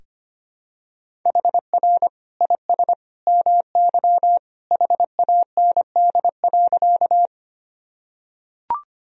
Morse code, heard as a keyed tone, about 25 wpm, 700 Hz. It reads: HR IS MY HAND.
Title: Sentences from Top 200 Words and Abbreviations 25wpm